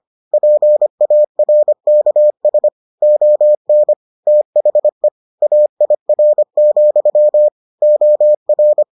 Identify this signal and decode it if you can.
Morse code, heard as a keyed tone, about 25 wpm, 600 Hz. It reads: PARKS ON THE AIR, OR